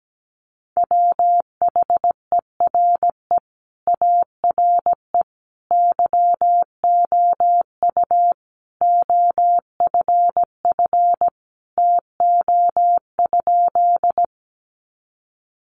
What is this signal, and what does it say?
Morse code, heard as a keyed tone, about 17 wpm, 700 Hz. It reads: WHERE ARE YOU OFF TO?
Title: Sentences from Top 200 Words 17wpm